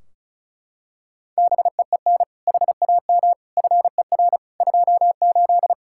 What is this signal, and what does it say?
Morse code, heard as a keyed tone, about 36 wpm, 700 Hz. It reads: BEEN HAM FER 28
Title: Practice QSO - Exchange 3 - More Info - 35wpm